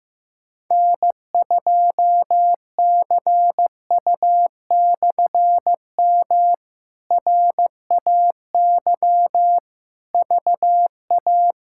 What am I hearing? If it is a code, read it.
Morse code, heard as a keyed tone, about 15 wpm, 700 Hz. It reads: N2CU/M RAY VA
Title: K1USN SST QSO Practice - 15wpm